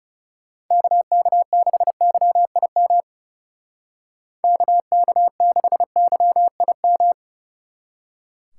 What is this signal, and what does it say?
Morse code, heard as a keyed tone, about 36 wpm, 700 Hz. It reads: KK6YIM KK6YIM
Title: Speed Racing - USA 2x3 Call Signs 30wpm